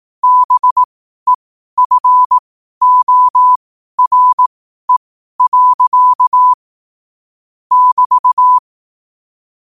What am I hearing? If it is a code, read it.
Morse code, heard as a keyed tone, about 18 wpm, spaced as for 12 wpm, 1000 Hz. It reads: BEFORE. <BT>
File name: qotdfiles / 12wpm